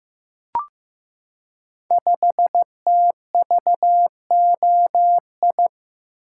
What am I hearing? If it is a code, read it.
Morse code, heard as a keyed tone, about 15 wpm, 700 Hz. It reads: 5TVOI
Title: Code Groups - 5 Letter:Numbers - 15wpm